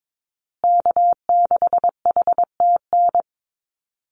Morse code, heard as a keyed tone, about 22 wpm, 700 Hz. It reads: K6HTN
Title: Call Signs Repeated 3x - 22wpm